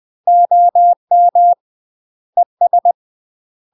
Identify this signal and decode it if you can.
Morse code, heard as a keyed tone, about 20 wpm, 700 Hz. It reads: OM ES
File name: UK MASTER QSOs TRAINING / 20WPM / UK MASTER QSOs 0066